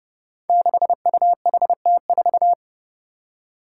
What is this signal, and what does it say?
Morse code, heard as a keyed tone, about 30 wpm, 700 Hz. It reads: 6UHT4
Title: Five Challenging Letter-Number Combinations 30wpm